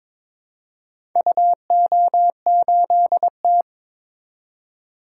Morse code, heard as a keyed tone, about 22 wpm, 700 Hz. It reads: UO8T
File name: practice / 22wpm / Four Letter-Number Combinations - 22wpm